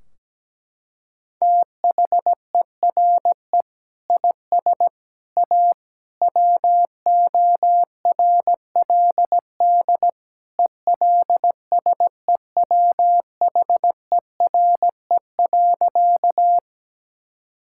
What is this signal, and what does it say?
Morse code, heard as a keyed tone, about 17 wpm, 700 Hz. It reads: THERE IS A WORLD ELSEWHERE.
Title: Sentences from Top 1600 Words 17wpm